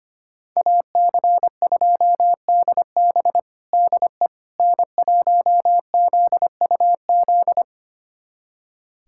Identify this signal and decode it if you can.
Morse code, heard as a keyed tone, about 25 wpm, 700 Hz. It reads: AC2DB DE N1ZUZ
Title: USA Callsign DE Callsign - 5 Characters - 25wpm